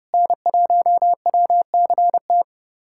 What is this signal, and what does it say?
Morse code, heard as a keyed tone, about 30 wpm, 700 Hz. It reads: N1WCT
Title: USA Callsign DE Callsign - 5 Characters - 30wpm